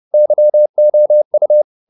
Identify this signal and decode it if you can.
Morse code, heard as a keyed tone, about 30 wpm, 600 Hz. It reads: YOU
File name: the old man and the sea / the old man and the sea 1 30wpm